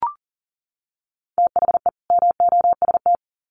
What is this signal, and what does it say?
Morse code, heard as a keyed tone, about 40 wpm, 700 Hz. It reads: THE MOST
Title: Sets of 2 Words 40wpm